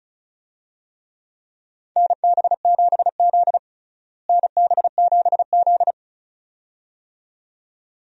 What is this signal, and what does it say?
Morse code, heard as a keyed tone, about 34 wpm, 700 Hz. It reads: NB7Z NB7Z
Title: USA Call Signs - 4 Characters - Repeated 2x - 35wpm